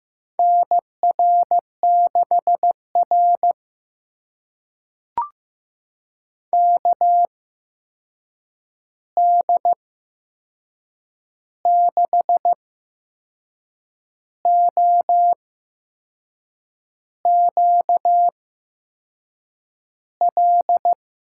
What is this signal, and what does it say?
Morse code, heard as a keyed tone, about 15 wpm, spaced as for 6 wpm, 700 Hz. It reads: NR6R KD6OQL
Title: USA Call Signs - 8x Character Spacing - 15wpm